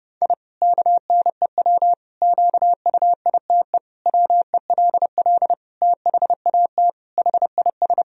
Morse code, heard as a keyed tone, about 30 wpm, 700 Hz. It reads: I KNEW QUITE WELL THAT HIS
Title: Sentences from Top 1000 Words - Review Part 3 - 30wpm